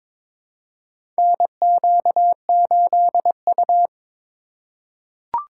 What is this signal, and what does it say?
Morse code, heard as a keyed tone, about 22 wpm, 700 Hz. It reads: NQ8U
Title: USA Call Signs - 4 Characters - 3x Character Spacing - 22wpm